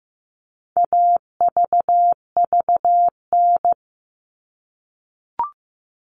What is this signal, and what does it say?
Morse code, heard as a keyed tone, about 15 wpm, 700 Hz. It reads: AVVN
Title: All Cut Numbers - 4-Digit 15wpm